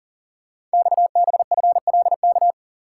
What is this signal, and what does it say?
Morse code, heard as a keyed tone, about 42 wpm, 700 Hz. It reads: XBFLK
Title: Code Groups - 5 Letters - 40wpm